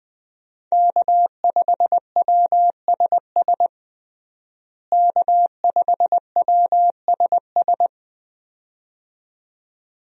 Morse code, heard as a keyed tone, about 20 wpm, 700 Hz. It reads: K5WSS K5WSS
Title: USA Call Signs - 5 Characters - Repeated 2x - 20wpm